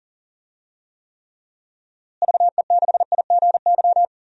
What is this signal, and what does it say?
Morse code, heard as a keyed tone, about 40 wpm, 700 Hz. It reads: VE6IGY
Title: DXCC-100 Call Signs - 6 Characters - Repeated 2x - 40wpm